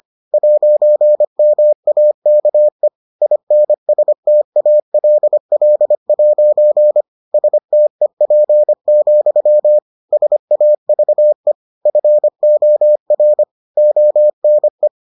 Morse code, heard as a keyed tone, about 25 wpm, 600 Hz. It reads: 'MAKE INSTALL' STEP, SAVE FOR ONE